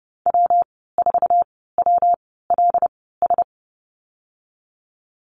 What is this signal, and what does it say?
Morse code, heard as a keyed tone, about 30 wpm, spaced as for 17 wpm, 700 Hz. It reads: W4WLS
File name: Lesson 19g - Letter w - Call Signs 30wpm